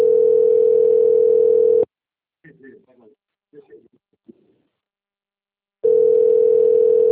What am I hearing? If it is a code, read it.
Ringback tone (North America), 440 + 480 Hz, with speech.
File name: out- - - - - .57653